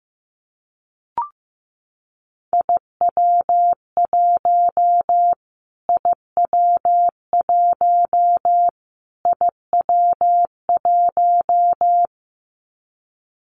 Morse code, heard as a keyed tone, about 15 wpm, 700 Hz. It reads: IW1 IW1 IW1
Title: Callsign Prefixes for Top 25 DXCC - Repeated 3x - 15wpm